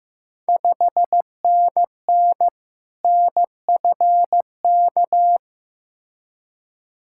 Morse code, heard as a keyed tone, about 15 wpm, 700 Hz. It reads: 5NN NFK
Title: Ontario QSO Party Exchange 15wpm